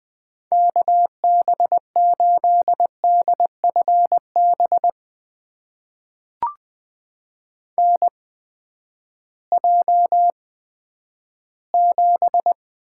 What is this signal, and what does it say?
Morse code, heard as a keyed tone, about 20 wpm, spaced as for 11 wpm, 700 Hz. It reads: KB8DFB NJ7
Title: USA Call Signs - 8x Character Spacing - 20wpm